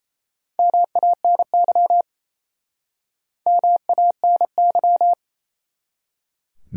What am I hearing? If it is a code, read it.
Morse code, heard as a keyed tone, about 40 wpm, 700 Hz. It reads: MANY MANY